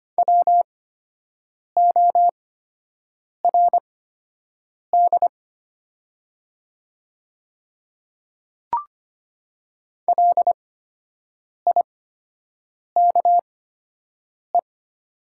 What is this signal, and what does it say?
Morse code, heard as a keyed tone, about 25 wpm, spaced as for 7 wpm, 700 Hz. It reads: WORD LIKE
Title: 8x Character Spacing - Top 100 Words - 25wpm